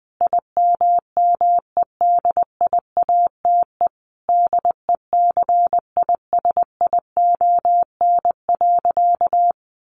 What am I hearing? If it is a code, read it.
Morse code, heard as a keyed tone, about 20 wpm, 700 Hz. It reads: IMMEDIATE DECISION.
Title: Sentences from Top 1200 Words 20wpm